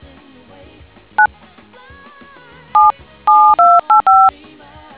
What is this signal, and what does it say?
DTMF, heard as telephone touch tones, tones of about 210 ms. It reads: 9 7 7285